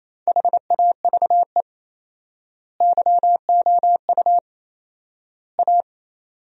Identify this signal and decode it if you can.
Morse code, heard as a keyed tone, about 28 wpm, 700 Hz. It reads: HAVE YOU A
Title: 4x Word-Spacing - Sentences from Top 400 Words - 28wpm